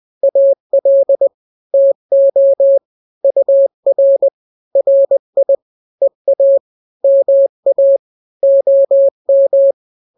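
Morse code, heard as a keyed tone, about 20 wpm, 550 Hz. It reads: AL TO UR RI EA MA OM
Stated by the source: Morse drill of two-letter n-grams at 20/18 WPM (Farnsworth), from KK7RR.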